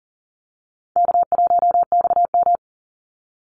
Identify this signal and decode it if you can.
Morse code, heard as a keyed tone, about 40 wpm, 700 Hz. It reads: K1XM
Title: Call Signs Repeated 2x - 40wpm